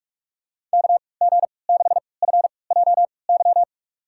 Morse code, heard as a keyed tone, about 46 wpm, spaced as for 25 wpm, 690 Hz. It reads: KG6FJY
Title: USA Call Signs - 6 Characters - 3x Character Spacing - 45wpm